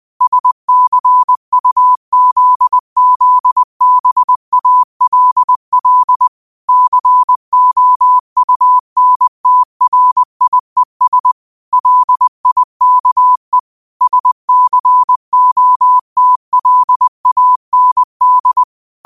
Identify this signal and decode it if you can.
Morse code, heard as a keyed tone, about 20 wpm, 1000 Hz. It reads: SCUZZBALL COUNTRIES LIKE SCOTLAND
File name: qotdfiles / 20wpm